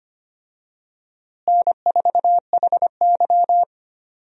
Morse code, heard as a keyed tone, about 25 wpm, 700 Hz. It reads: N4HY
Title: USA Call Signs - 4 Characters - Repeated 2x - 25wpm